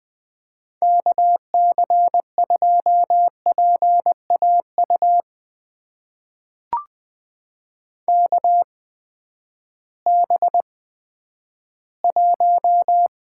Morse code, heard as a keyed tone, about 20 wpm, spaced as for 11 wpm, 700 Hz. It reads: KC2PAU KB1
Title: USA Call Signs - 6 Characters - 8x Character Spacing - 20wpm